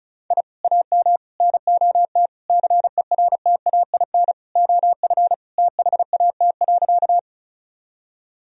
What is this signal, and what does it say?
Morse code, heard as a keyed tone, about 36 wpm, 700 Hz. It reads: I AM NOT CERTAIN OF THAT.
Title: Sentences from Top 1000 Words - Review Part 2 - 35wpm